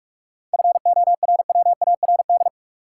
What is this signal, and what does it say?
Morse code, heard as a keyed tone, about 41 wpm, 700 Hz. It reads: FORWARD
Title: Sentences from Top 1300 Words - New Word Drill - 45wpm